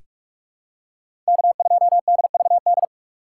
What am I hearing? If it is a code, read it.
Morse code, heard as a keyed tone, about 45 wpm, 690 Hz. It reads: K2DVD
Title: USA Call Signs - 5 Characters - 6x Character Spacing - 45wpm